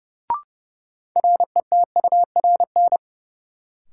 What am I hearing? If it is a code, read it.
Morse code, heard as a keyed tone, about 30 wpm, 700 Hz. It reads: RETURN
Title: Sentences from Top 800 Words - New Word Drill - 30wpm